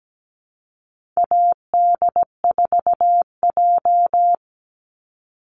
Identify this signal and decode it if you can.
Morse code, heard as a keyed tone, about 17 wpm, 700 Hz. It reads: AD4J
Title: Call Signs Repeated 3x - 17wpm